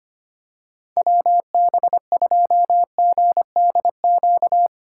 Morse code, heard as a keyed tone, about 25 wpm, 700 Hz. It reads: WB2GDQ
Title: USA Call Signs - 6 Characters - 3x Character Spacing - 25wpm